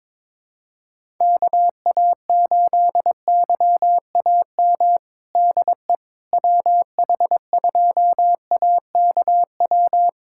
Morse code, heard as a keyed tone, about 22 wpm, 700 Hz. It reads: KA8YAM DE WH2AKW
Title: USA Callsign DE Callsign - 6 Characters - 22wpm